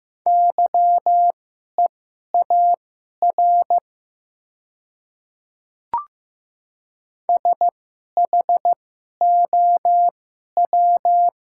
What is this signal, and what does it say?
Morse code, heard as a keyed tone, about 15 wpm, spaced as for 11 wpm, 700 Hz. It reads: YEAR SHOW
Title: 2x Character Spacing - Top 200 Words - 15wpm